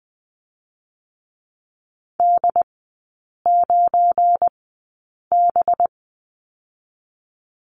Morse code, heard as a keyed tone, about 20 wpm, 700 Hz. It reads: D 9 B
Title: 184 Lesson 30e - Letter b - Three Characters 20wpm